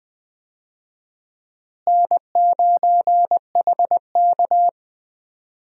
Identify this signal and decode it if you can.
Morse code, heard as a keyed tone, about 20 wpm, 700 Hz. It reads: N9HK
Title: USA Call Signs - 4 Characters - Repeated 3x - 20wpm